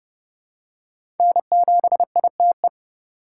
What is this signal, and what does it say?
Morse code, heard as a keyed tone, about 30 wpm, 700 Hz. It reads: N7ITE
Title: USA Call Signs - 5 Characters - 2x Character Spacing - 30wpm